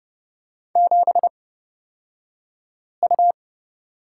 Morse code, heard as a keyed tone, about 31 wpm, spaced as for 5 wpm, 700 Hz. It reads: 7U
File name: Lesson 36d - BT - Two Characters 30wpm